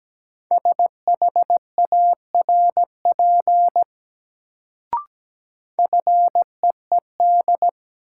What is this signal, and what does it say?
Morse code, heard as a keyed tone, about 17 wpm, 700 Hz. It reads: SHARP FEED
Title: Sentences from Top 1600 Words - New Word Drill Alt - 17wpm